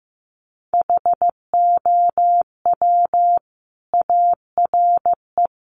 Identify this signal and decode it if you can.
Morse code, heard as a keyed tone, about 15 wpm, 700 Hz. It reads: HOW ARE
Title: Sentences from Top 1100 Words 15wpm